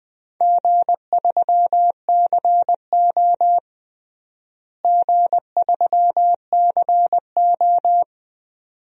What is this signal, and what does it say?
Morse code, heard as a keyed tone, about 20 wpm, 700 Hz. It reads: G3CO G3CO
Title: DXCC-100 Call Signs - 4 Characters - Repeated 2x - 20wpm